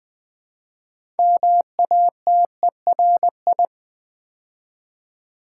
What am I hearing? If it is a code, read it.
Morse code, heard as a keyed tone, about 20 wpm, 700 Hz. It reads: MATERI
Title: Build A Word - 2 Letters At A Time - 8-Letter Words - 20wpm